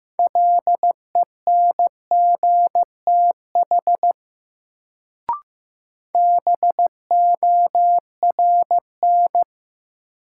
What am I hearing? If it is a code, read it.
Morse code, heard as a keyed tone, about 15 wpm, 700 Hz. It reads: LENGTH BORN
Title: Sentences from Top 1000 Words - New Word Drill - 15wpm